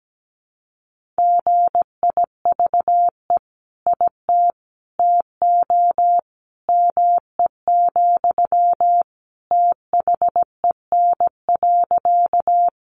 Morse code, heard as a keyed tone, about 17 wpm, 700 Hz. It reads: GIVE IT TO ME, THEN.
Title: Sentences from Top 200 Words 17wpm